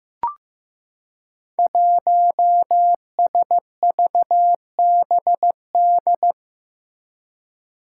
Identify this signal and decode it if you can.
Morse code, heard as a keyed tone, about 15 wpm, 700 Hz. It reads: 1SVBD
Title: Five Challenging Letter-Number Combinations 15wpm